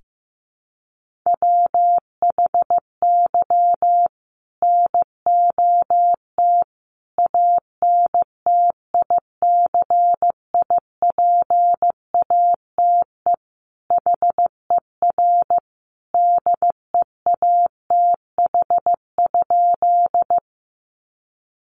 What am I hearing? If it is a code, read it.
Morse code, heard as a keyed tone, about 15 wpm, 700 Hz. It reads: WHY NOT ANTICIPATE HER DEATH?
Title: Sentences from Top 1500 Words 15wpm